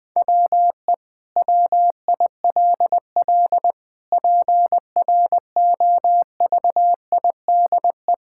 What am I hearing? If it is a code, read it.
Morse code, heard as a keyed tone, about 20 wpm, 700 Hz. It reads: WE WILL PROVIDE
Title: Sentences from Top 1000 Words - Review Part 5 - 20wpm